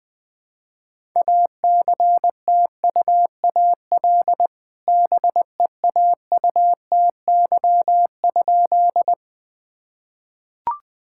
Morse code, heard as a keyed tone, about 20 wpm, 700 Hz. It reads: ACTUAL BEAUTY?
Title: Sentences from Top 1000 Words - Review Part 5 - 20wpm